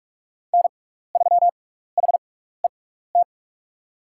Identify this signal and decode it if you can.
Morse code, heard as a keyed tone, about 48 wpm, spaced as for 16 wpm, 700 Hz. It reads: N3HET
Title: USA Call Signs - 5 Characters - 6x Character Spacing - 45wpm